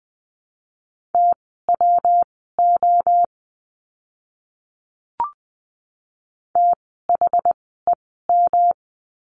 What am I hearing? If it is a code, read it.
Morse code, heard as a keyed tone, about 20 wpm, spaced as for 14 wpm, 700 Hz. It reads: TWO THEM
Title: 160 Lesson 26f - Letter g - Words 20wpm